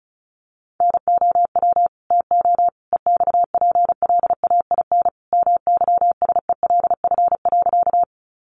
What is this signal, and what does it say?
Morse code, heard as a keyed tone, about 34 wpm, 700 Hz. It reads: NOW TO EXPLAIN MYSELF.
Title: Sentences from Top 1300 Words 35wpm